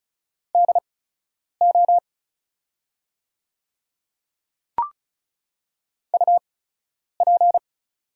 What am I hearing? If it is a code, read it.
Morse code, heard as a keyed tone, about 36 wpm, spaced as for 10 wpm, 700 Hz. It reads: DO UP
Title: 8x Character Spacing - Top 100 Words - 35wpm